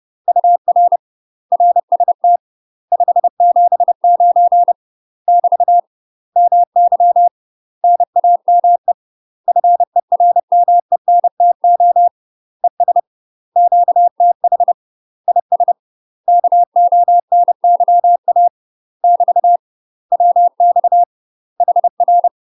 Morse code, heard as a keyed tone, about 30 wpm, 700 Hz. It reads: UR RST 579 <BT> MY NAME FERMENTO ES QTH IS KONYA <BT> WX HR
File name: UK MASTER QSOs TRAINING / 30WPM / UK MASTER QSOs 0043